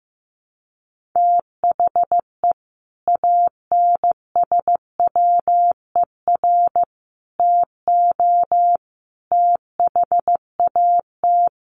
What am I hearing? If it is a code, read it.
Morse code, heard as a keyed tone, about 15 wpm, 700 Hz. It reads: THE ANSWER TO THAT
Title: Sets of 5 Words Encore2 15wpm